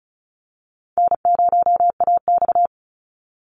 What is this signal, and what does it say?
Morse code, heard as a keyed tone, about 34 wpm, 700 Hz. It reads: N0AX
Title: Easy Call Signs Repeated 3x - 35wpm